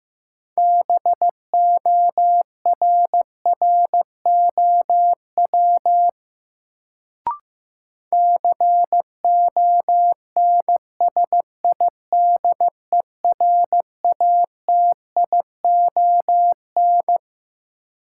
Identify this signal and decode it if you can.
Morse code, heard as a keyed tone, about 15 wpm, 700 Hz. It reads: BORROW CONSIDERATION
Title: Sentences from Top 1500 Words - New Word Drill - 15wpm